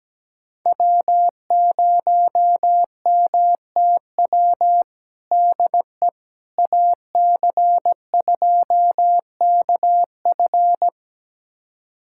Morse code, heard as a keyed tone, about 17 wpm, 700 Hz. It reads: W0MTW DE AC2KF
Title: USA Callsign DE Callsign - 5 Characters - 17wpm